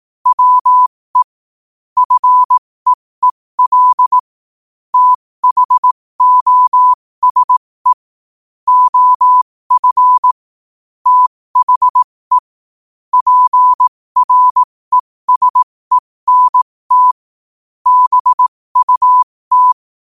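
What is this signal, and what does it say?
Morse code, heard as a keyed tone, about 18 wpm, spaced as for 15 wpm, 1000 Hz. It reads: WE FEEL THOSE OF THE PRESENT BUT
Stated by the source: Morse code quote of the day sent at 15 words per minute.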